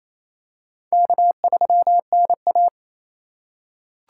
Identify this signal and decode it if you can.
Morse code, heard as a keyed tone, about 28 wpm, 700 Hz. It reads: K3NA
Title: USA Call Signs - 4 Characters - 3x Character Spacing - 28wpm